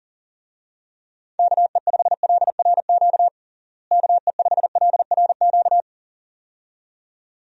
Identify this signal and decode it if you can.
Morse code, heard as a keyed tone, about 40 wpm, 700 Hz. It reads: KE5LRQ KE5LRQ
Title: USA Call Signs - 6 Characters - Repeated 2x - 40wpm